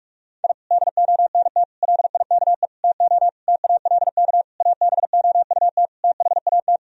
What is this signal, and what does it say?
Morse code, heard as a keyed tone, about 46 wpm, 700 Hz. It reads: I DONT LIKE TO TALK ABOUT THAT
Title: Sentences from Top 1000 Words - Review Part 1 - 45wpm